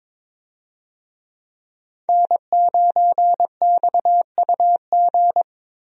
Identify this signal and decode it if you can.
Morse code, heard as a keyed tone, about 22 wpm, 700 Hz. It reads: N9XUG
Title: USA Call Signs - 5 Characters - Repeated 3x - 22wpm